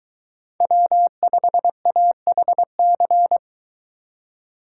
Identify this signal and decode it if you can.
Morse code, heard as a keyed tone, about 22 wpm, 700 Hz. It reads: W5AHC